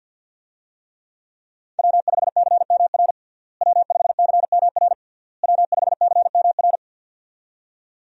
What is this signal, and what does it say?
Morse code, heard as a keyed tone, about 49 wpm, 700 Hz. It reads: W5CMR W5CMR W5CMR
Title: USA Call Signs - 5 Characters - Repeated 3x - 50wpm